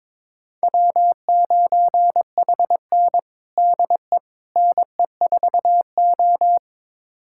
Morse code, heard as a keyed tone, about 22 wpm, 700 Hz. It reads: W9HN DE NE4O
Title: USA Callsign DE Callsign - 4 Characters - 22wpm